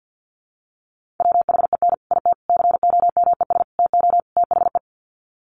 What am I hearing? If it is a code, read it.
Morse code, heard as a keyed tone, about 50 wpm, 700 Hz. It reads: WHEN IT COMES TO THE
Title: Sets of 5 Words 50wpm